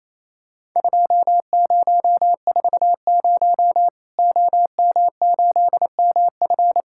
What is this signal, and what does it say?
Morse code, heard as a keyed tone, about 28 wpm, 700 Hz. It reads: 2040 OM8MF
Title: Worked All Europe DX Contest - QTC Exchange - 28wpm